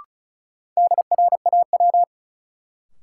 Morse code, heard as a keyed tone, about 36 wpm, 700 Hz. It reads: DRAW
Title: Sentences from Top 300 Words - New Word Drill - 35wpm